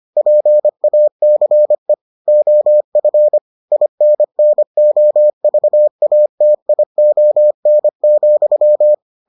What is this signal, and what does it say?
Morse code, heard as a keyed tone, about 25 wpm, 600 Hz. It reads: PACE OF INNOVATION,